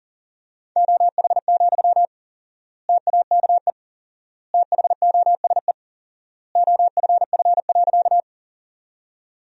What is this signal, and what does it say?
Morse code, heard as a keyed tone, about 40 wpm, 700 Hz. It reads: OH, TAKE THOSE OFF.
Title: 4x Word-Spacing - Sentences from Top 500 Words - 40wpm